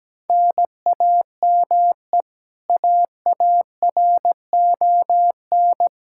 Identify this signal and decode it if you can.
Morse code, heard as a keyed tone, about 17 wpm, 700 Hz. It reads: NAME AARON
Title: Practice QSO - Exchange 1 - Beginning - 17wpm